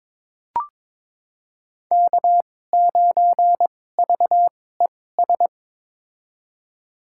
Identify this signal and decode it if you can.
Morse code, heard as a keyed tone, about 22 wpm, spaced as for 16 wpm, 700 Hz. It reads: K9VES
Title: USA Call Signs - 5 Characters - 2x Character Spacing - 22wpm